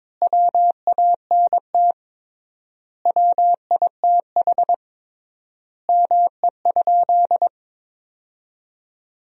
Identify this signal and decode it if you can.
Morse code, heard as a keyed tone, about 22 wpm, 700 Hz. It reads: WANT WITH ME?
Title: 3x Word-Spacing - Sentences from Top 200 Words - 22wpm